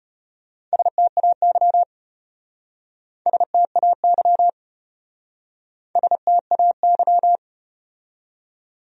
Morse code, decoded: STAY STAY STAY

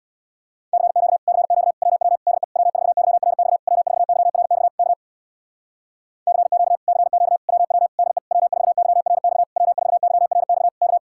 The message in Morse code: CQ CQ WW DE W4YRX W4YRX K CQ CQ WW DE W4YRX W4YRX K

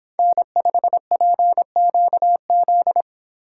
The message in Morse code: N5PQZ

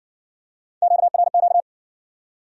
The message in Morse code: QRQ

